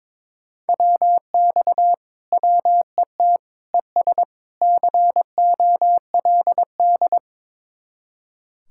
Morse code, heard as a keyed tone, about 22 wpm, 700 Hz. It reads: WX WET ES COLD
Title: Speed-Racing Common QSO Phrases - 22wpm